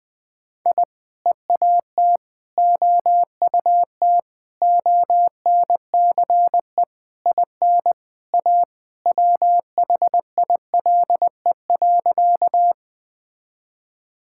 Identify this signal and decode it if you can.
Morse code, heard as a keyed tone, about 19 wpm, 700 Hz. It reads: I EAT OUT ONCE IN A WHILE.
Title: Sentences with 4-Word Phrases - 20wpm